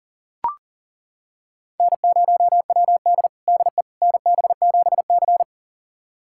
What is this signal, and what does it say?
Morse code, heard as a keyed tone, about 40 wpm, 700 Hz. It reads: N0WD DE NB7C
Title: USA Callsign DE Callsign - 4 Characters - 40wpm